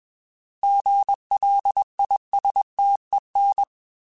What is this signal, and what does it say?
Morse code, heard as a keyed tone, about 20 wpm, 800 Hz. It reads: GLISTEN
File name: Alphabetical / Alphabetical mp3s / Gs @20